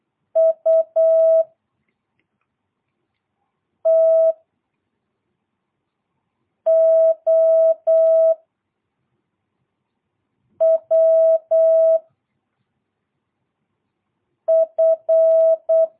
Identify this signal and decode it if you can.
Morse code, heard as a keyed tone, about 8 wpm, 660 Hz. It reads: U T O W F
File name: jh8cbh / 2022cw